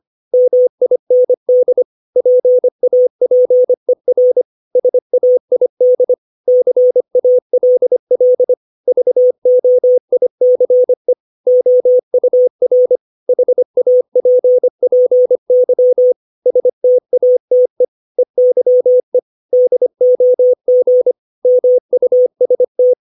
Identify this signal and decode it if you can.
Morse code, heard as a keyed tone, about 25 wpm, 500 Hz. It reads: MIND PAPER SAID CALL VOICE OUR HAPPY STATE EYE DOG MUST
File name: 632 words / 632 Words 25WPM